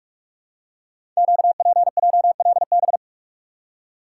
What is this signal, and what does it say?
Morse code, heard as a keyed tone, about 44 wpm, 690 Hz. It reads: QPJLB
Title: Code Groups - 5 Letters - 45wpm